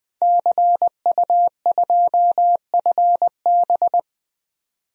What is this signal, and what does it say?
Morse code, heard as a keyed tone, about 20 wpm, 700 Hz. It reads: CU2FB
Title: DXCC-100 Call Signs - 5 Characters - 20wpm